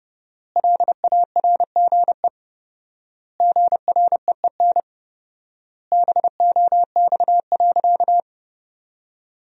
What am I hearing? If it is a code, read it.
Morse code, heard as a keyed tone, about 30 wpm, 700 Hz. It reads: LARGE GREEN BOX.
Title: 4x Word-Spacing - Sentences from Top 500 Words - 30wpm